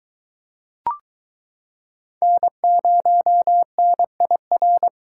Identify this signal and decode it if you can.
Morse code, heard as a keyed tone, about 23 wpm, 700 Hz. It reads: N0NIR